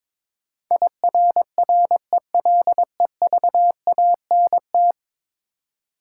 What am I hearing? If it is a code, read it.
Morse code, heard as a keyed tone, about 22 wpm, 700 Hz. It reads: IRRELEVANT
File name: practice / 22wpm / 10 Letter Words - First 3 Letters Given - 22wpm